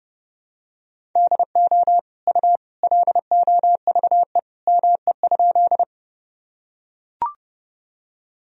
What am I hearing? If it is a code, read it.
Morse code, heard as a keyed tone, about 30 wpm, 700 Hz. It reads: DO U LOVE ME?
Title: Sentences from Top 400 Words and Abbreviations 30wpm